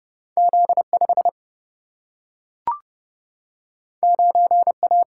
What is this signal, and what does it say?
Morse code, heard as a keyed tone, about 30 wpm, 700 Hz. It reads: Z5 9A
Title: Single-Letter Single-Number 30wpm